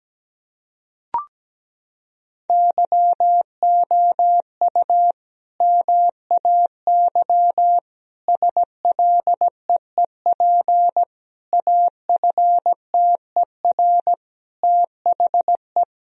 Morse code, heard as a keyed tone, about 17 wpm, 700 Hz. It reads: YOU MAY SLEEP AFTER THE
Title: Sentences from Top 1000 Words - Review Part 4 - 17wpm